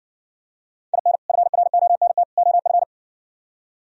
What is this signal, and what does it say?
Morse code, heard as a keyed tone, about 58 wpm, 700 Hz. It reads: IN FRONT OF